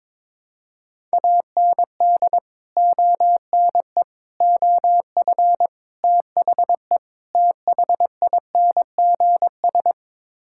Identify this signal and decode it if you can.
Morse code, heard as a keyed tone, about 22 wpm, 700 Hz. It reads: AND ONE OF THE THINGS
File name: practice / 22wpm / Build A Phrase - 5-Word Phrases - 22wpm